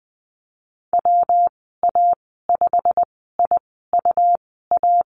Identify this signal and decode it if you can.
Morse code, heard as a keyed tone, about 20 wpm, spaced as for 15 wpm, 700 Hz. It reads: WA5IUA
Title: 128 Lesson 21g - Number 6 - Call Signs 20wpm